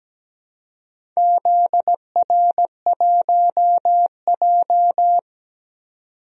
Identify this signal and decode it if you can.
Morse code, heard as a keyed tone, about 17 wpm, 700 Hz. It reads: ZR1J
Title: DXCC-100 Call Signs - 4 Characters - 17wpm